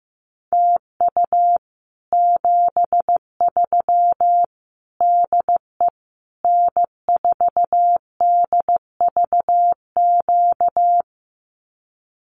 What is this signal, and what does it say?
Morse code, heard as a keyed tone, about 15 wpm, 700 Hz. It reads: TU 73 DE N4DVQ
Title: Common QSO Phrases Encore 15wpm